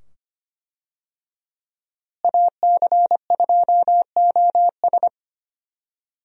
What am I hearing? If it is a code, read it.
Morse code, heard as a keyed tone, about 25 wpm, 700 Hz. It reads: AC2OS
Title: USA Call Signs - 5 Characters - Repeated 2x - 25wpm